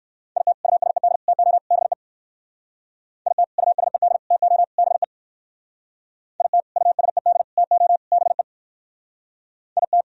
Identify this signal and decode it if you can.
Morse code, heard as a keyed tone, about 58 wpm, 700 Hz. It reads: IT USED TO BE IT USED TO BE IT USED TO BE IT